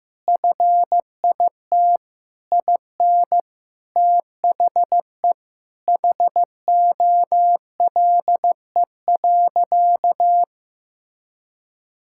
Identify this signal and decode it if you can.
Morse code, heard as a keyed tone, about 15 wpm, 700 Hz. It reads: FIT IN THE HOLE.
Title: Sentences from Top 1600 Words 15wpm